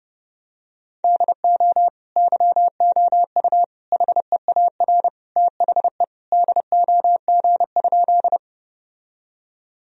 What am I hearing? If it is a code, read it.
Morse code, heard as a keyed tone, about 30 wpm, 700 Hz. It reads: DO YOU HEAR THE DOG?
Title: Sentences from Top 1000 Words - Review Part 2 - 30wpm